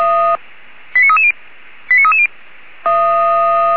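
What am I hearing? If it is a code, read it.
Five-tone selective call (ZVEI), heard as quick runs of five tones, tones of about 70 ms. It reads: 89330 89330
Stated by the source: ZVEI five-tone call 89315 received by SDR, from feuerwehr-orlamuende.org fire brigade recording server.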